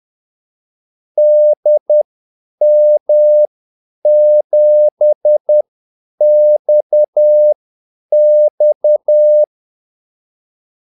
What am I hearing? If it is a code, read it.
Morse code, heard as a keyed tone, about 10 wpm, spaced as for 8 wpm, 600 Hz. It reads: DM7XX
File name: Callsigns-10-8wpm